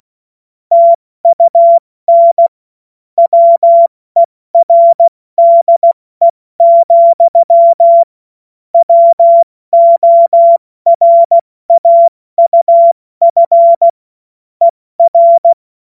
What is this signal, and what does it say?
Morse code, heard as a keyed tone, about 16 wpm, 680 Hz. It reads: TUN WERDE, WORAUF ER
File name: Morsekurs / Witze / Witz 30. August 2022 70 F80